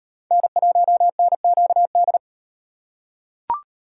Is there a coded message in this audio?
Yes: Morse code from 0.3 to 2.2 s, heard as a keyed tone, about 38 wpm, 700 Hz. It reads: N1NQD